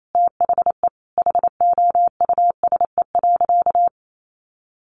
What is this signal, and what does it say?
Morse code, heard as a keyed tone, about 28 wpm, 700 Hz. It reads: THE HOUSE.
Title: Sentences from Top 1500 Words 28wpm